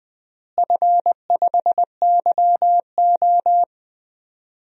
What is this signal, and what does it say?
Morse code, heard as a keyed tone, about 20 wpm, 700 Hz. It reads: F5YO
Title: DXCC-100 Call Signs - 4 Characters - Repeated 2x - 20wpm